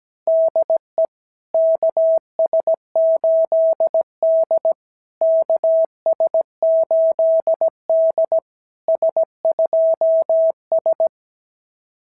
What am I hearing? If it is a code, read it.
Morse code, heard as a keyed tone, about 17 wpm, 660 Hz. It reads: DE KS8D KS8D S2S
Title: SOTA QSO Practice 17wpm